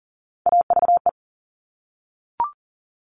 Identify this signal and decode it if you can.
Morse code, heard as a keyed tone, about 42 wpm, 700 Hz. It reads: AVE
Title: Top Three Letter Combinations in Words 40wpm